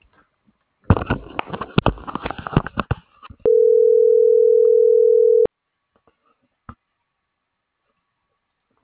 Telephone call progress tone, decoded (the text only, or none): ringback tone